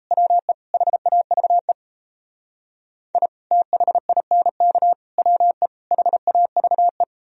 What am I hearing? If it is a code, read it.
Morse code, heard as a keyed tone, about 35 wpm, 700 Hz. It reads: WE HAVE I THINK WE HAVE
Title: Speed-Racing - Sets of 4 Words 28wpm